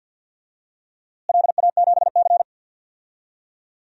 Morse code, heard as a keyed tone, about 46 wpm, 700 Hz. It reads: LA7C
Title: DXCC-100 Call Signs - 4 Characters - Repeated 2x - 50wpm